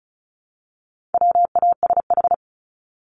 Morse code, heard as a keyed tone, about 34 wpm, 700 Hz. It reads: WASH
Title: Top 1751-2000 Words 35wpm